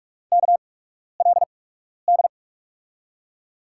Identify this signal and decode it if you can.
Morse code, heard as a keyed tone, about 44 wpm, spaced as for 12 wpm, 700 Hz. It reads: KLD
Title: USA Call Signs - 6 Characters - 8x Character Spacing - 45wpm